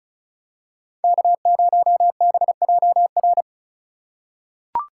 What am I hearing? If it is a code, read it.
Morse code, heard as a keyed tone, about 36 wpm, 700 Hz. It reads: K0BJR
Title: USA Call Signs - 5 Characters - 3x Character Spacing - 35wpm